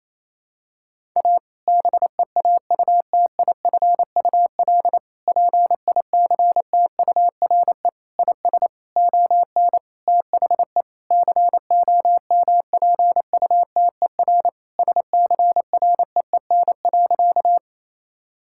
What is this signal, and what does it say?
Morse code, heard as a keyed tone, about 28 wpm, 700 Hz. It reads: A BEAUTIFUL PICTURE IS ON THE COMPUTER SCREEN.
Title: Sentences from Top 1000 Words - Review Part 6 - 28wpm